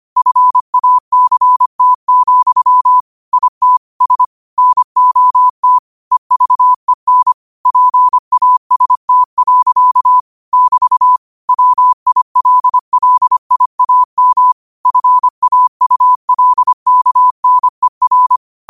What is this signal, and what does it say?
Morse code, heard as a keyed tone, about 25 wpm, 1000 Hz. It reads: FACT, IT S NOT EVEN PAST. <BT> WILLIAM FAULKNER
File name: qotdfiles / 25wpm